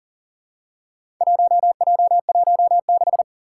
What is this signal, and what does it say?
Morse code, heard as a keyed tone, about 40 wpm, 700 Hz. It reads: 1J16
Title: Four Challenging Letter-Number Combinations 40wpm